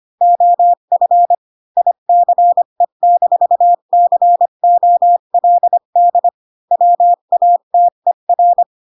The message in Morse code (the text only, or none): OF ICE-COLD WATER